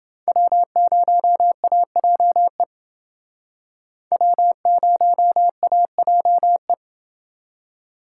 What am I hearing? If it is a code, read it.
Morse code, heard as a keyed tone, about 29 wpm, 700 Hz. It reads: W0AJE W0AJE